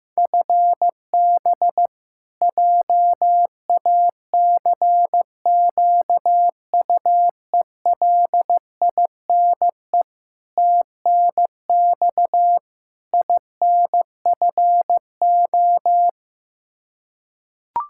Morse code, heard as a keyed tone, about 15 wpm, 700 Hz. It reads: FB JACQUELINE TNX INFO
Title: Practice QSO - Exchange 3 - More Info - 15wpm